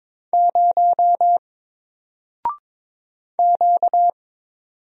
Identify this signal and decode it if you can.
Morse code, heard as a keyed tone, about 22 wpm, 700 Hz. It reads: 0 Q
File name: Single-Letter-Number-22wpm